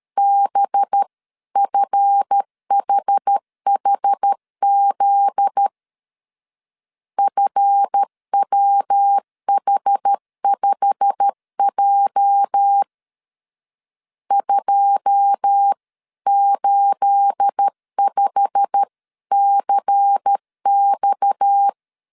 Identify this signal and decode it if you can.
Morse code, heard as a keyed tone, about 13 wpm, 800 Hz. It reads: BFHHZ FWH5J 285CX
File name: CW / DET / 1 9 1500C S C 12wpm 3811